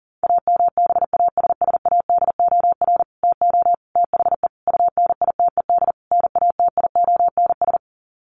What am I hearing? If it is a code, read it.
Morse code, heard as a keyed tone, about 40 wpm, 700 Hz. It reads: AMBASSADOR TO THE UNITED NATIONS
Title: Sets of 5 Words Encore2 40wpm